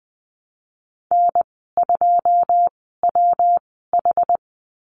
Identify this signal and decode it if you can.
Morse code, heard as a keyed tone, about 20 wpm, spaced as for 14 wpm, 700 Hz. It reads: N2WH
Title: 140 Lesson 23g - Letter f - Call Signs 20wpm